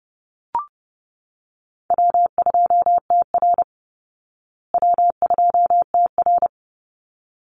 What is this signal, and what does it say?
Morse code, heard as a keyed tone, about 30 wpm, 700 Hz. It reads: W2TR W2TR
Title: Easy Call Signs Repeated 2x - 30wpm